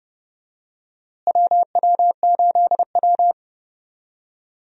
Morse code, heard as a keyed tone, about 30 wpm, 700 Hz. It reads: WW8W
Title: USA Call Signs - 4 Characters - Repeated 3x - 30wpm